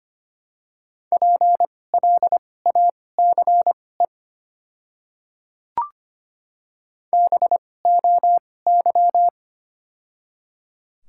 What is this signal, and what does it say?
Morse code, heard as a keyed tone, about 25 wpm, spaced as for 18 wpm, 700 Hz. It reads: PLACE BOY
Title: 2x Character Spacing - Top 200 Words - 25wpm